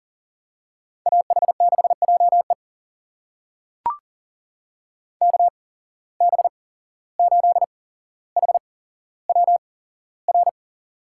Morse code, heard as a keyed tone, about 38 wpm, spaced as for 16 wpm, 690 Hz. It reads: AH6JE KB8HWR
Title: USA Call Signs - 8x Character Spacing - 40wpm